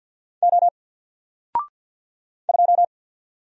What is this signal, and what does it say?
Morse code, heard as a keyed tone, about 48 wpm, 700 Hz. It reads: O 2
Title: Single Letter-Number Alt 50wpm